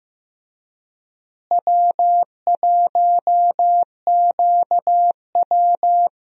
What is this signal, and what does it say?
Morse code, heard as a keyed tone, about 15 wpm, 700 Hz. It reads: W1QW
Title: USA Call Signs - 4 Characters - Repeated 2x - 15wpm